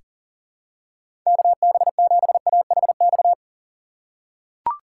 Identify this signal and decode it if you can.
Morse code, heard as a keyed tone, about 40 wpm, 700 Hz. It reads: KB7AHX
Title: USA Call Signs - 6 Characters - 4x Character Spacing - 40wpm